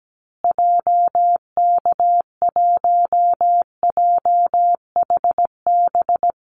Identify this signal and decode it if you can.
Morse code, heard as a keyed tone, about 17 wpm, 700 Hz. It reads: JK1JHB
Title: Call Signs Repeated 2x - 17wpm